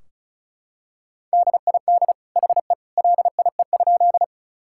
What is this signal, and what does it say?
Morse code, heard as a keyed tone, about 36 wpm, 700 Hz. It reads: DID HE LIE?
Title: Sentences from Top 1600 Words 35wpm